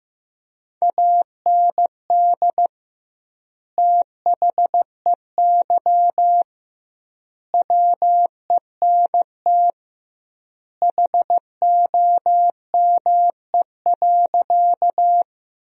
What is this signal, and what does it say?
Morse code, heard as a keyed tone, about 15 wpm, 700 Hz. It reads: AND THEY WENT HOME.
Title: 2x Word-Spacing - Sentences from Top 200 Words - 15wpm